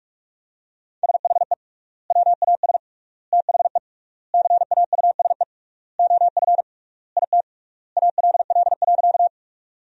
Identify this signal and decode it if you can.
Morse code, heard as a keyed tone, about 46 wpm, 700 Hz. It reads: SHE WAS THE CAUSE OF IT ALL.
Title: 3x Word-Spacing - Sentences from Top 200 Words - 45wpm